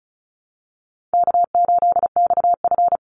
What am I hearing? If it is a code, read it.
Morse code, heard as a keyed tone, about 34 wpm, 700 Hz. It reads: K8XF
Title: Call Signs Repeated 3x - 35wpm